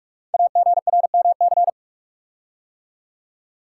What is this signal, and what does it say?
Morse code, heard as a keyed tone, about 44 wpm, 690 Hz. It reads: AGRMC
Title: Code Groups - 5 Letters - 45wpm